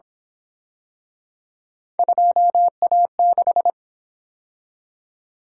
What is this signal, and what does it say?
Morse code, heard as a keyed tone, about 28 wpm, 700 Hz. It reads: 2A6